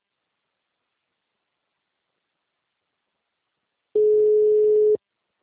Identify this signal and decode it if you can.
Ringback tone (Europe, 425 Hz).
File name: out- -2002- - - .0